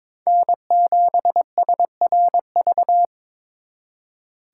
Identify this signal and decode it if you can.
Morse code, heard as a keyed tone, about 26 wpm, 700 Hz. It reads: N7SRV
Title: Speed Racing - USA 1x3 Call Signs 22wpm